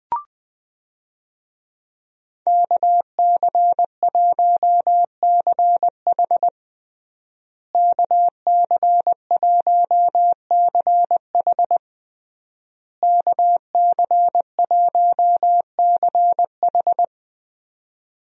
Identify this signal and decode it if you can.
Morse code, heard as a keyed tone, about 20 wpm, 700 Hz. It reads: KC1CH KC1CH KC1CH
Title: USA Call Signs - 5 Characters - Repeated 3x - 20wpm